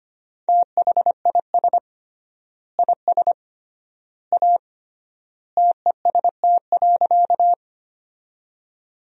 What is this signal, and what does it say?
Morse code, heard as a keyed tone, about 25 wpm, 700 Hz. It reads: THIS IS A TEST.
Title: 3x Word-Spacing - Sentences from Top 500 Words - 25wpm